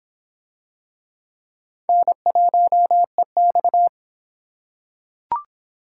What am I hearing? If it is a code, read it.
Morse code, heard as a keyed tone, about 26 wpm, 700 Hz. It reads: N1EX